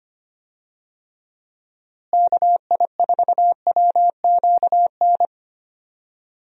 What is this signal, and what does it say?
Morse code, heard as a keyed tone, about 25 wpm, 700 Hz. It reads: KI4WQN
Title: USA Call Signs - 6 Characters - Repeated 2x - 25wpm